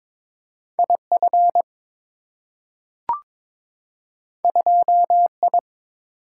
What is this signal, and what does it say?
Morse code, heard as a keyed tone, about 22 wpm, 690 Hz. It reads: IF 2I
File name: practice / 22wpm / Two Letter-Number Combinations - 22wpm